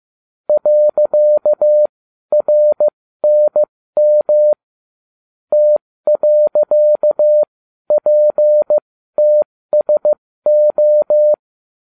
Morse code, heard as a keyed tone, about 15 wpm, spaced as for 12 wpm, 600 Hz. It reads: .RNM T.PTSO